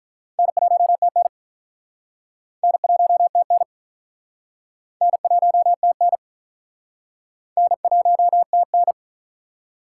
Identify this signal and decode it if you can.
Morse code, heard as a keyed tone, about 44 wpm, 700 Hz. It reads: N1TN N1TN N1TN N1TN
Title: Speed Racing - USA 1x2 Call Signs 35wpm